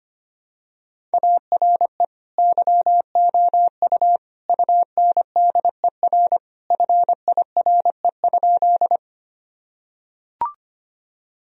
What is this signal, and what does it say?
Morse code, heard as a keyed tone, about 25 wpm, 700 Hz. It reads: ARE YOU UNDER FIRE?
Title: Sentences from Top 1000 Words - Review Part 1 - 25wpm